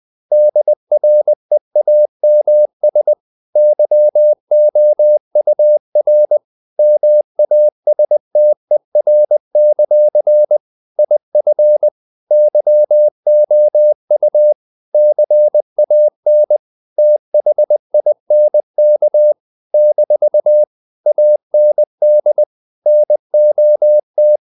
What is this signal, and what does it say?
Morse code, heard as a keyed tone, about 20 wpm, 600 Hz. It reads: DREAMS YOUR MASTER; IF YOU CAN THINK - AND NOT
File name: CW / Practice / If20